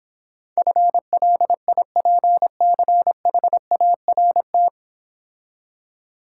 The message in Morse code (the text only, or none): FLIPCHART